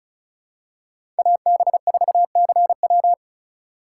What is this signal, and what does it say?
Morse code, heard as a keyed tone, about 36 wpm, 700 Hz. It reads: AB4CW
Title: USA Call Signs - 5 Characters - 8x Character Spacing - 35wpm